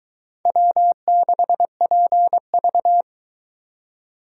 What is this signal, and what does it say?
Morse code, heard as a keyed tone, about 24 wpm, 700 Hz. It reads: W6PV